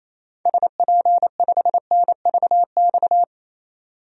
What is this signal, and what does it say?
Morse code, heard as a keyed tone, about 28 wpm, 700 Hz. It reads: SP5NVX
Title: DXCC-100 Call Signs - 6 Characters - 28wpm